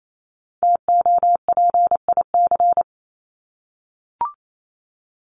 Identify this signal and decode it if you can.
Morse code, heard as a keyed tone, about 28 wpm, 700 Hz. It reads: TOPIC
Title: Top 1001-1250 Words 28wpm